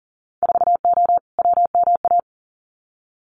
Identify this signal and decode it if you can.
Morse code, heard as a keyed tone, about 38 wpm, 700 Hz. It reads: 4O WMA
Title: Winter Field Day Exchanges Encore 40wpm